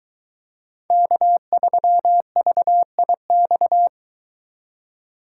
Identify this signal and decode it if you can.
Morse code, heard as a keyed tone, about 22 wpm, 700 Hz. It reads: K3VIX